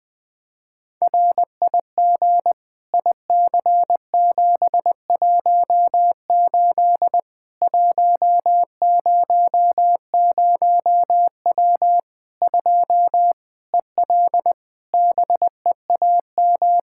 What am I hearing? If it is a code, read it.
Morse code, heard as a keyed tone, about 20 wpm, 700 Hz. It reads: RIG IC718 100W 2 EL BEAM
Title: Practice QSO - Exchange 2 - Info - 20wpm